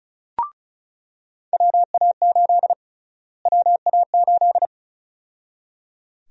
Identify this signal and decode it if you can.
Morse code, heard as a keyed tone, about 36 wpm, 700 Hz. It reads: WA8 WA8
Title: USA Call Sign Prefixes - Repeated 2x - 35wpm